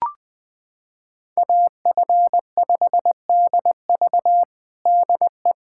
Morse code, heard as a keyed tone, about 21 wpm, 700 Hz. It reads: AF5DV DE
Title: USA Callsign DE Callsign - 5 Characters - 20wpm